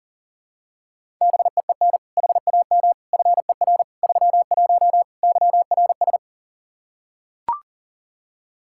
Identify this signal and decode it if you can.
Morse code, heard as a keyed tone, about 37 wpm, 700 Hz. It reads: BEEN HAM FER 31 YRS
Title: K1USN SST QSO Practice 40wpm